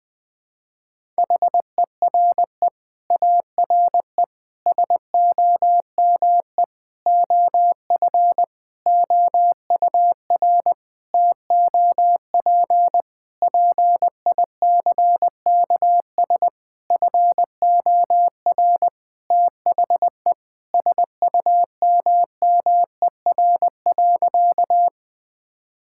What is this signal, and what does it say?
Morse code, heard as a keyed tone, about 20 wpm, 700 Hz. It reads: HERE ARE SOME OF OUR TOP PICKS FOR THE SUMMER.
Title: Sentences with 9-Word Phrases - 20wpm